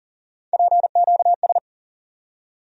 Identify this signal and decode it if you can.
Morse code, heard as a keyed tone, about 40 wpm, 700 Hz. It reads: PQS
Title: USA Call Sign Postfixes - 3 Letters 40wpm